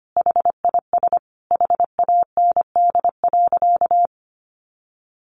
Morse code, heard as a keyed tone, about 25 wpm, 700 Hz. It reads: HIS HAND.
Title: Sentences from Top 1200 Words 25wpm